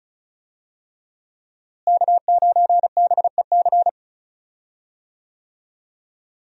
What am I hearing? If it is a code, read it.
Morse code, heard as a keyed tone, about 34 wpm, 700 Hz. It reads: K9BEC
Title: USA Call Signs - 5 Characters - Repeated 3x - 35wpm